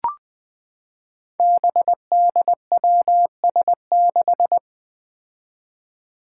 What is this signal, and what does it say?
Morse code, heard as a keyed tone, about 20 wpm, 700 Hz. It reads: BDWS6
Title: Five Challenging Letter-Number Combinations 20wpm